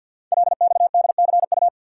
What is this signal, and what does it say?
Morse code, heard as a keyed tone, about 50 wpm, 690 Hz. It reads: LXDCU
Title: Code Groups - 5 Letters - 50wpm